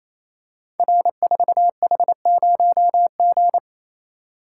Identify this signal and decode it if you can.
Morse code, heard as a keyed tone, about 28 wpm, 700 Hz. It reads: R4H0G
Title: Code Groups - 5 Letter:Numbers - 28wpm